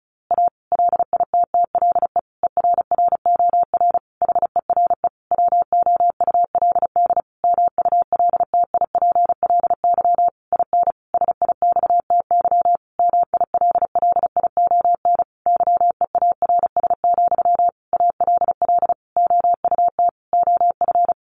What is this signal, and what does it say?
Morse code, heard as a keyed tone, about 34 wpm, 700 Hz. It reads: A LITTLE ERROR HERE WOULD MULTIPLY IN SIXTY MILLION YEARS, ALL OUT OF
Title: A Sound Of Thunder 35wpm